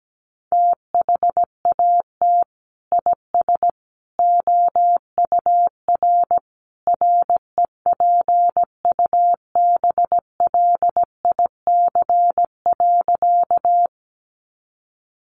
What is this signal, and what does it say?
Morse code, heard as a keyed tone, about 17 wpm, 700 Hz. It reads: THAT IS OUR REPUBLIC.
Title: Sentences from Top 1500 Words 17wpm